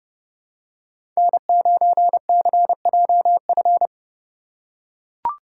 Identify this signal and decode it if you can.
Morse code, heard as a keyed tone, about 32 wpm, 700 Hz. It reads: N9CJF
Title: USA Call Signs - 5 Characters - 4x Character Spacing - 30wpm